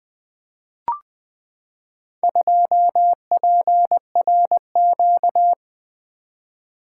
Morse code, heard as a keyed tone, about 20 wpm, 700 Hz. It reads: 2PRQ
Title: Four Letter-Number Combinations - 20wpm